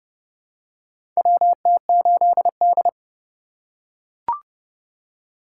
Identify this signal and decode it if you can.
Morse code, heard as a keyed tone, about 30 wpm, 700 Hz. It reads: WT8D
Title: USA Call Signs - 4 Characters - 3x Character Spacing - 30wpm